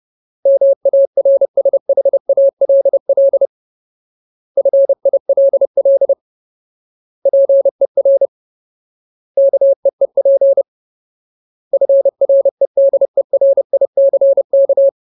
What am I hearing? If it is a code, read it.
Morse code, heard as a keyed tone, about 30 wpm, 550 Hz. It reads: MARSHALL FILL PER KEEP FREDERICK
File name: Ultimate Word List / 30WPM Ultimate